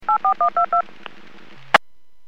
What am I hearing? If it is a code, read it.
DTMF, heard as telephone touch tones, tones of about 100 ms. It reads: 84122
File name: WPS / 00762 17-07 WPS 451.25 WPS